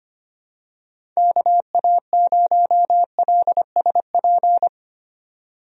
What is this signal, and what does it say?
Morse code, heard as a keyed tone, about 25 wpm, 700 Hz. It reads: KA0LSP
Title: USA Call Signs - 6 Characters - 25wpm